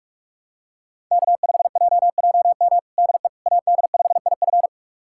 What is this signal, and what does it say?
Morse code, heard as a keyed tone, about 45 wpm, 700 Hz. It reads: K5JJM DE AD5IF
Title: USA Callsign DE Callsign - 5 Characters - 45wpm